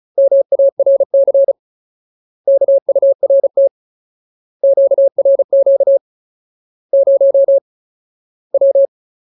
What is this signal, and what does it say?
Morse code, heard as a keyed tone, about 34 wpm, 550 Hz. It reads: MARC KURT QRQ 0 W
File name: Ultimate Word List / 35WPM Ultimate